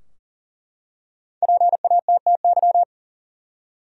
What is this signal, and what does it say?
Morse code, heard as a keyed tone, about 40 wpm, 700 Hz. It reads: PATTY
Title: USA Common Names - 40wpm